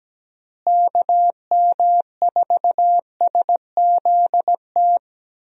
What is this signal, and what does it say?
Morse code, heard as a keyed tone, about 17 wpm, 700 Hz. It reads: KM4SZT
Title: USA Call Signs - 6 Characters - Repeated 3x - 17wpm